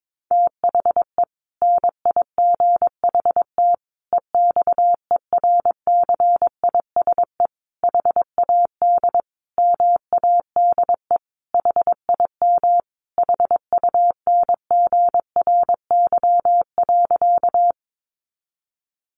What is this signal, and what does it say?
Morse code, heard as a keyed tone, about 22 wpm, 700 Hz. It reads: THE NIGHT EXERCISE HAD MADE HIM HUNGRY.
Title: Sentences from Top 1400 Words 22wpm